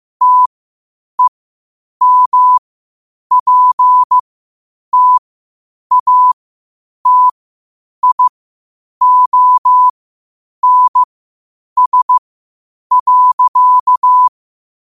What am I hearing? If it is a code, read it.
Morse code, heard as a keyed tone, about 15 wpm, spaced as for 8 wpm, 1000 Hz. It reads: TEMPTATIONS.
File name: qotdfiles / 8wpm